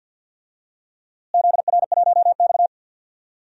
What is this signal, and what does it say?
Morse code, heard as a keyed tone, about 46 wpm, 700 Hz. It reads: ZR1X
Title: DXCC-100 Call Signs - 4 Characters - Repeated 3x - 50wpm